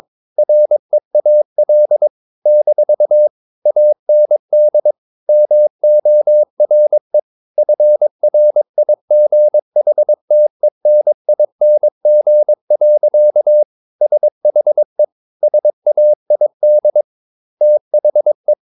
Morse code, heard as a keyed tone, about 22 wpm, 600 Hz. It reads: REAL - AND MORE FRIGHTENING. SHE SAID THE
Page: Romanen Never af Ken Follet på engelsk.